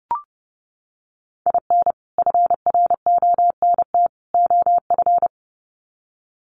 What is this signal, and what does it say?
Morse code, heard as a keyed tone, about 30 wpm, 700 Hz. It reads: IN FRONT OF
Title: Sets of 3 Words 30wpm